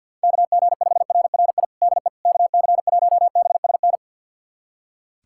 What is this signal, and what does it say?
Morse code, heard as a keyed tone, about 51 wpm, 700 Hz. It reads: KG5RRI DE KC1BSN
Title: USA Callsign DE Callsign 50wpm